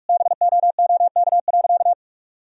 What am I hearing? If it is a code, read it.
Morse code, heard as a keyed tone, about 41 wpm, 700 Hz. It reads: BOOK.
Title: Sentences from Top 1000 Words - Review Part 4 - 45wpm